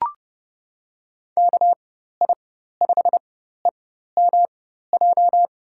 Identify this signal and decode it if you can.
Morse code, heard as a keyed tone, about 30 wpm, spaced as for 14 wpm, 700 Hz. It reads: KI5EMJ
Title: USA Call Signs - 4x Character Spacing - 30wpm